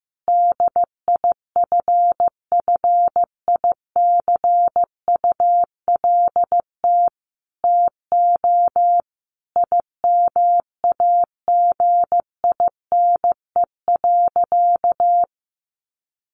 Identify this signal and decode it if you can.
Morse code, heard as a keyed tone, about 15 wpm, 700 Hz. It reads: DIFFICULT TO IMAGINE.
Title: Sentences from Top 1400 Words 15wpm